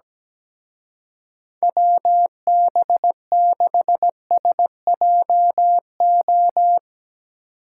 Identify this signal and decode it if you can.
Morse code, heard as a keyed tone, about 18 wpm, 700 Hz. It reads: WB6SJO
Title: Speed Racing - USA Call Signs 17wpm